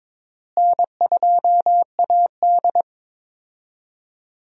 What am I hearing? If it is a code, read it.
Morse code, heard as a keyed tone, about 22 wpm, 700 Hz. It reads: N2AD
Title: USA Call Signs - 4 Characters - Repeated 3x - 22wpm